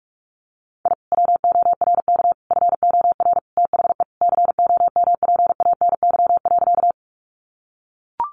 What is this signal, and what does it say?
Morse code, heard as a keyed tone, about 44 wpm, 700 Hz. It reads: I WORK FOR THE COMPANY.
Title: Sentences from Top 600 Words 45wpm